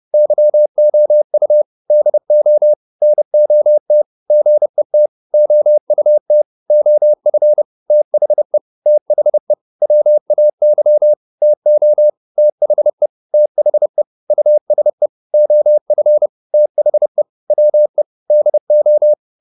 Morse code, heard as a keyed tone, about 30 wpm, 600 Hz. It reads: YOU DO NOT GET OUT OF THE THE WAY TO THE THE USE OF THE WE DO
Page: advanced-practice-files